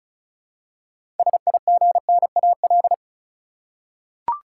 Morse code, heard as a keyed tone, about 34 wpm, 700 Hz. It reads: SIGNAL
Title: Six Letter Words - 35wpm